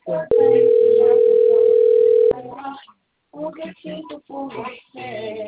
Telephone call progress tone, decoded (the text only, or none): ringback tone